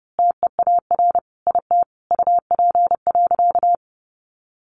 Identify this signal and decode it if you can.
Morse code, heard as a keyed tone, about 30 wpm, 700 Hz. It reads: TEAR IT UP.
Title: A Sound Of Thunder 30wpm